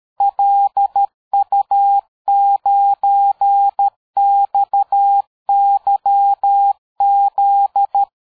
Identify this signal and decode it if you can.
Morse code, heard as a keyed tone, about 13 wpm, 800 Hz. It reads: LU9XYZ
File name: dk5ke / QSO 60BpM